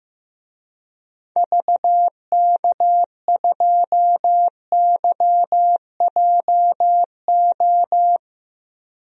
Morse code, heard as a keyed tone, about 15 wpm, 700 Hz. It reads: VK2YJO
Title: DXCC-100 Call Signs - 6 Characters - Repeated 2x - 15wpm